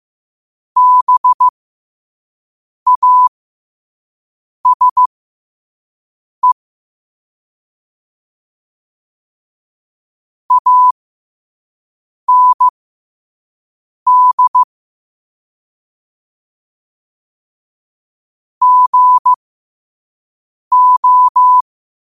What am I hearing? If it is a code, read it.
Morse code, heard as a keyed tone, about 15 wpm, spaced as for 5 wpm, 1000 Hz. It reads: BASE AND GO
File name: qotdfiles / 5wpm